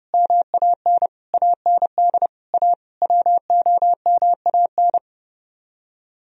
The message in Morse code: MAN AND A WOMAN

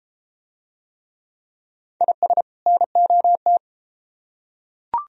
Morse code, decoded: IS NOT

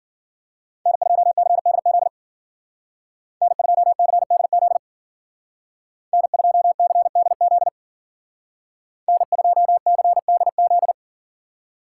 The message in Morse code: N2CDZ N2CDZ N2CDZ N2CDZ

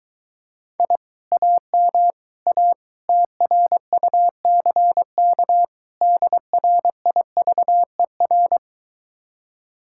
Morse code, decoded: I AM A TRUCK DRIVER

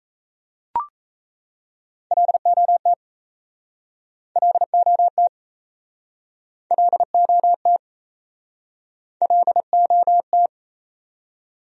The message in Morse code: LOT LOT LOT LOT

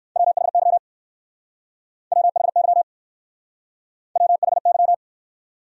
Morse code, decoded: WHY WHY WHY

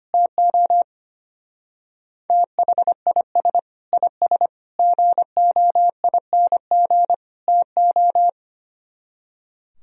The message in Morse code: TO THIS IS GOING TO

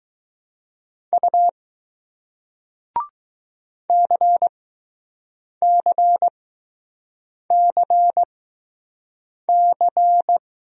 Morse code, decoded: U CCCC

